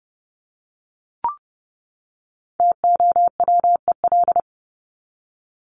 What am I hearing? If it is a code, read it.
Morse code, heard as a keyed tone, about 30 wpm, 700 Hz. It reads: TOWEL
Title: Top 1501-1750 Words 30wpm